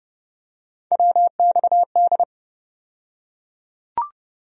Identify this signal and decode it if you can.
Morse code, heard as a keyed tone, about 30 wpm, 700 Hz. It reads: WXD
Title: USA Call Sign Postfixes - 3 Letters 30wpm